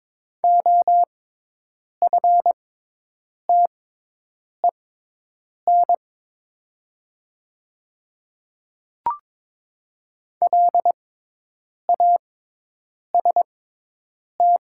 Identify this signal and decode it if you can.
Morse code, heard as a keyed tone, about 22 wpm, spaced as for 8 wpm, 700 Hz. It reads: OFTEN LAST
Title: 6x Character Spacing - Top 300 Words - 22wpm